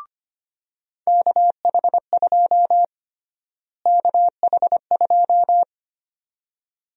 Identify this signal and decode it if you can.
Morse code, heard as a keyed tone, about 25 wpm, 700 Hz. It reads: KH2 KH2
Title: USA Call Sign Prefixes - Repeated 2x - 25wpm